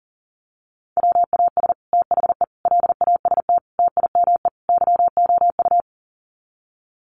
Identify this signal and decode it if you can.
Morse code, heard as a keyed tone, about 40 wpm, 700 Hz. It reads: WAS THE LAST TIME YOU
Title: Sets of 5 Words Encore2 40wpm